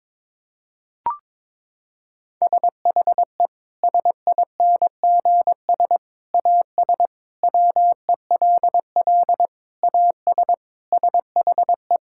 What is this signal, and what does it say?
Morse code, heard as a keyed tone, about 22 wpm, 700 Hz. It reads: SHE SINGS AS WELL AS SHE
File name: practice / 22wpm / Sentences with 3-Word Phrases - 22wpm